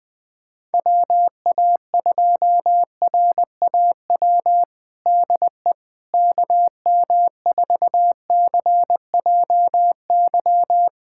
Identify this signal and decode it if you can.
Morse code, heard as a keyed tone, about 20 wpm, 700 Hz. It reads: WA2RAW DE KM4CJY
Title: USA Callsign DE Callsign - 6 Characters - 20wpm